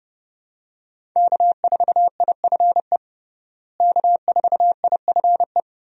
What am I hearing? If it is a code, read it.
Morse code, heard as a keyed tone, about 30 wpm, 700 Hz. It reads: K4IFE K4IFE
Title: USA Call Signs - 5 Characters - Repeated 2x - 30wpm